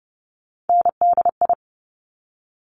Morse code, heard as a keyed tone, about 30 wpm, 700 Hz. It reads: NDI
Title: Top Three Letter Combinations in Words 30wpm